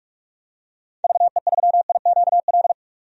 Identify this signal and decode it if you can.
Morse code, heard as a keyed tone, about 45 wpm, 700 Hz. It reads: VE3IQL
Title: DXCC-100 Call Signs - 6 Characters - 45wpm